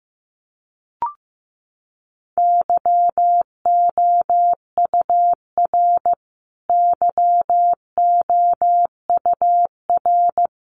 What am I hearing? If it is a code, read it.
Morse code, heard as a keyed tone, about 15 wpm, 700 Hz. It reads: YOUR YOUR
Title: Top 500 Words Repeated 15wpm